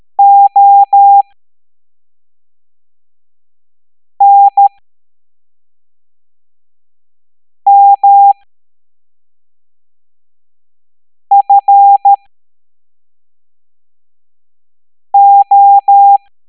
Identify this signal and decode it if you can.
Morse code, heard as a keyed tone, about 13 wpm, spaced as for 3 wpm, 800 Hz. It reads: ONMFO
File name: hamdata / L09-EX04